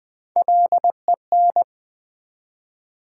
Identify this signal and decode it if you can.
Morse code, heard as a keyed tone, about 21 wpm, 700 Hz. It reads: LEN
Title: Build A Word - 1 Letter At A Time - 4-Letter Words - 20wpm